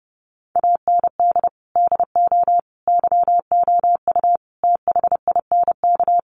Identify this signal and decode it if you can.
Morse code, heard as a keyed tone, about 30 wpm, 700 Hz. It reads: AND DO YOU THINK
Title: Sentences from Top 600 Words 30wpm